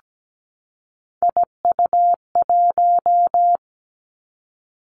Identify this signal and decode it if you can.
Morse code, heard as a keyed tone, about 17 wpm, 700 Hz. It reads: IU1
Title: Three Challenging Letter-Number Combinations 17wpm